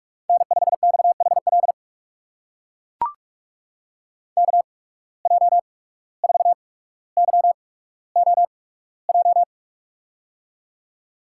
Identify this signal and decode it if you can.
Morse code, heard as a keyed tone, about 44 wpm, spaced as for 18 wpm, 700 Hz. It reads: N5XHL KJ4YOJ
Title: USA Call Signs - 8x Character Spacing - 45wpm